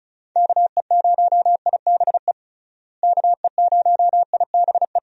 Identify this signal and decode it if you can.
Morse code, heard as a keyed tone, about 36 wpm, 700 Hz. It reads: KE0IBE KE0IBE
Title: USA Call Signs - 6 Characters - Repeated 3x - 35wpm